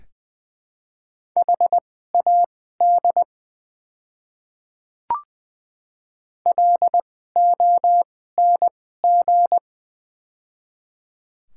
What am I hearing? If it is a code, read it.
Morse code, heard as a keyed tone, about 20 wpm, spaced as for 15 wpm, 700 Hz. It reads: HAD LONG
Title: 2x Character Spacing - Top 100 Words - 20wpm